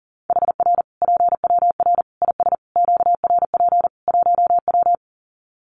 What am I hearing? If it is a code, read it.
Morse code, heard as a keyed tone, about 40 wpm, 700 Hz. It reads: HR PWR IS QRP 1W
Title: Common QSO Phrases 40wpm